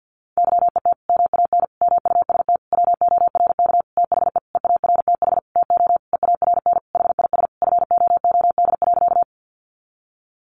Morse code, heard as a keyed tone, about 50 wpm, 700 Hz. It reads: YET MAN MUST WORK THE EARTH TO EARN HIS FOOD.
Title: Sentences from Top 1500 Words 50wpm